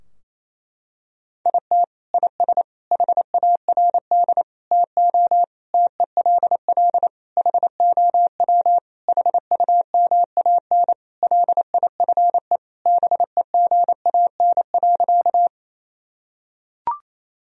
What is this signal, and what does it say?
Morse code, heard as a keyed tone, about 28 wpm, 700 Hz. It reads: IT IS HARD TO TELL HOW HUMAN LIFE BEGAN.
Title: Sentences from Top 1000 Words - Review Part 3 - 28wpm